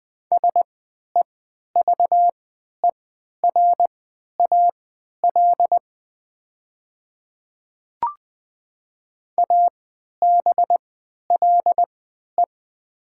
Morse code, heard as a keyed tone, about 20 wpm, spaced as for 11 wpm, 700 Hz. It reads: SEVERAL ABLE
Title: 3x Character Spacing - Top 500 Words - 20wpm